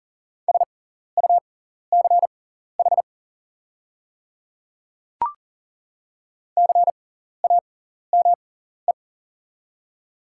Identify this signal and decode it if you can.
Morse code, heard as a keyed tone, about 40 wpm, spaced as for 14 wpm, 690 Hz. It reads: SUCH CAME
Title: 6x Character Spacing - Top 500 Words - 40wpm